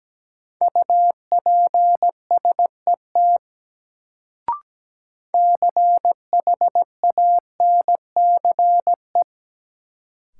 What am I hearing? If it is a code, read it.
Morse code, heard as a keyed tone, about 17 wpm, 700 Hz. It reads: UPSET CHANCE
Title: Sentences from Top 800 Words - New Word Drill - 17wpm